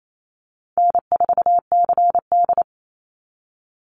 Morse code, heard as a keyed tone, about 28 wpm, 700 Hz. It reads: N4CD
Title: Call Signs Repeated 2x - 28wpm